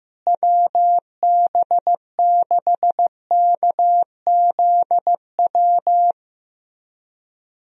Morse code, heard as a keyed tone, about 16 wpm, 700 Hz. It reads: WB6KZW
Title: USA Call Signs - 6 Characters - Repeated 3x - 15wpm